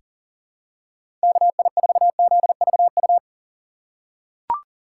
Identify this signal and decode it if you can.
Morse code, heard as a keyed tone, about 40 wpm, 700 Hz. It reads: KI4ZVU
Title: K1USN SST QSO Practice - 40wpm